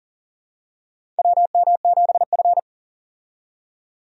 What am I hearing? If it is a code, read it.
Morse code, heard as a keyed tone, about 40 wpm, 700 Hz. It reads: WM7F
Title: USA Call Signs - 4 Characters - 40wpm